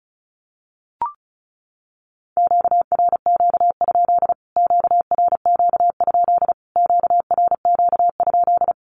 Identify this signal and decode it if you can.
Morse code, heard as a keyed tone, about 34 wpm, 700 Hz. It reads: QRQ? QRQ? QRQ?
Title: Q-Codes Repeated 3x 35wpm